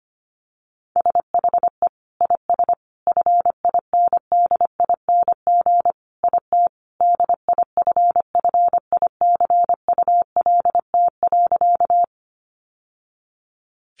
Morse code, heard as a keyed tone, about 25 wpm, 700 Hz. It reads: SHE IS FINDING IT DIFFICULT.
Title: Sentences from Top 1100 Words 25wpm